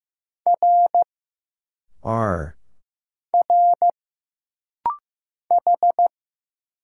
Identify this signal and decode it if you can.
Morse code, heard as a keyed tone, about 15 wpm, 700 Hz. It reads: R R H
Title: Single Letters - Rapid-Fire 15wpm